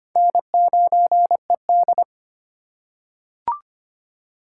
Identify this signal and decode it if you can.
Morse code, heard as a keyed tone, about 25 wpm, 700 Hz. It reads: N9ED
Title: USA Call Signs - 4 Characters - 4x Character Spacing - 25wpm